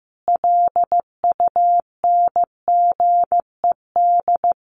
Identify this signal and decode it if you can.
Morse code, heard as a keyed tone, about 15 wpm, 700 Hz. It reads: LUNGED
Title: A Sound Of Thunder 15wpm